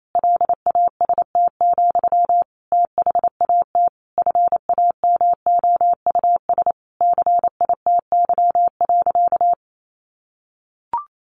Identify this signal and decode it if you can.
Morse code, heard as a keyed tone, about 28 wpm, 700 Hz. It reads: LAST, THAT FAMOUS CITY.
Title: Sentences from Top 1500 Words 28wpm